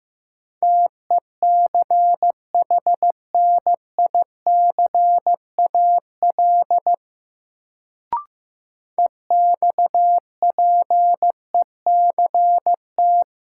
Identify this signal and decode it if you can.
Morse code, heard as a keyed tone, about 15 wpm, 700 Hz. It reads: TECHNICAL EXPECT
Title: Sentences from Top 1100 Words - New Word Drill - 15wpm